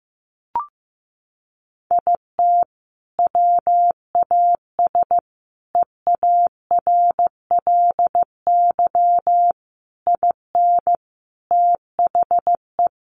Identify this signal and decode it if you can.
Morse code, heard as a keyed tone, about 15 wpm, 700 Hz. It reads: IT WAS EARLY IN THE
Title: Sentences from Top 1200 Words 15wpm